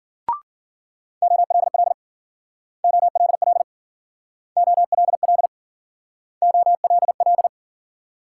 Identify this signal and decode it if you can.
Morse code, heard as a keyed tone, about 50 wpm, 700 Hz. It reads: OLL OLL OLL OLL
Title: Speed-Racing Multi-letter Phonemes - 40wpm